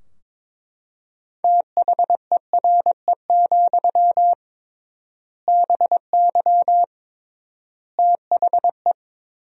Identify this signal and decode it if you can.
Morse code, heard as a keyed tone, about 22 wpm, 700 Hz. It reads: THERE, BY THE
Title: 3x Word-Spacing - Sentences from Top 500 Words - 22wpm